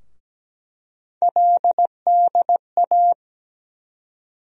Morse code, heard as a keyed tone, about 17 wpm, 700 Hz. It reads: LDA
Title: USA Call Sign Postfixes - 3 Letters 17wpm